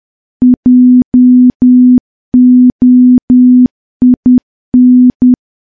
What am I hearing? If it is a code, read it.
Morse code, heard as a keyed tone, about 10 wpm, 250 Hz. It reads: JOIN
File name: morse / morse-training-10WPM